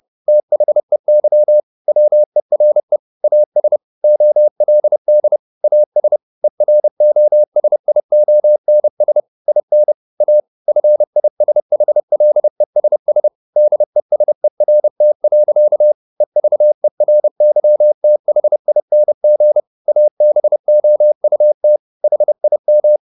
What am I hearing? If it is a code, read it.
Morse code, heard as a keyed tone, about 30 wpm, 600 Hz. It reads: THEY WERE AS OLD AS EROSIONS IN A FISHLESS DESERT. EVERYTHING ABOUT HIM
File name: the old man and the sea / the old man and the sea 1 30wpm